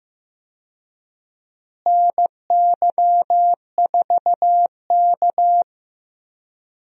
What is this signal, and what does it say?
Morse code, heard as a keyed tone, about 15 wpm, 700 Hz. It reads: NY4K
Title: USA Call Signs - 4 Characters - Repeated 3x - 15wpm